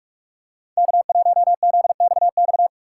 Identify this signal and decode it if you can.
Morse code, heard as a keyed tone, about 44 wpm, 690 Hz. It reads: K1ZXX
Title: USA Call Signs - 6x Character Spacing - 45wpm